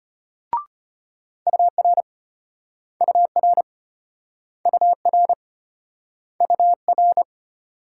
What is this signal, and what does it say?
Morse code, heard as a keyed tone, about 34 wpm, 700 Hz. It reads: UR UR UR UR